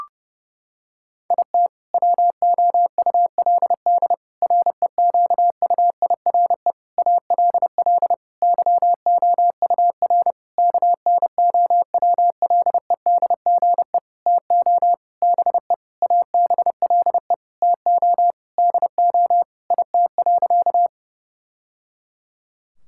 Morse code, heard as a keyed tone, about 32 wpm, 700 Hz. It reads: IT WOULD REQUIRE ALL YOUR KNOWLEDGE TO BE ABLE TO DO IT.
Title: Sentences from Top 1000 Words - Review Part 6 - 30wpm